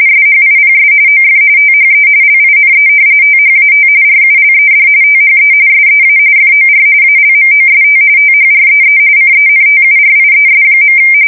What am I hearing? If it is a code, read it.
RTTY, about 45.45 baud. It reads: SHIPS OF THE LINE THAT RULED THE SEA FOR ALMOST THREE CENTURIES.
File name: fsk170 / battleshipnj